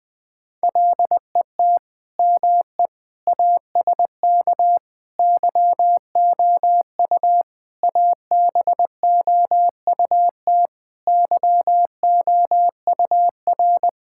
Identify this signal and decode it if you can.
Morse code, heard as a keyed tone, about 20 wpm, 700 Hz. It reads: LET ME ASK YOU ABOUT YOUR
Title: Sentences with 5-Word Phrases - Spoken Beginning - 20wpm